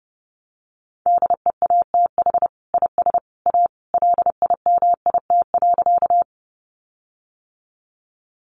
Morse code, heard as a keyed tone, about 30 wpm, 700 Hz. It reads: DEATH IS A LIMIT.
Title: Sentences from Top 1500 Words 30wpm